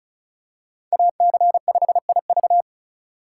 Morse code, heard as a keyed tone, about 36 wpm, 700 Hz. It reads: AC5IV
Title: USA Call Signs - 5 Characters - 4x Character Spacing - 35wpm